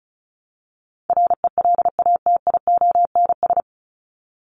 Morse code, heard as a keyed tone, about 34 wpm, 700 Hz. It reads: RELATIONS
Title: ES:S Suffix - Top 100 Words with Base Words 35wpm